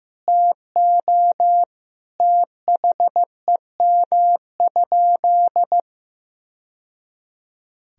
Morse code, heard as a keyed tone, about 15 wpm, 700 Hz. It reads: TO THEM?
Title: Sentences from Top 1000 Words - Review Part 2 - 15wpm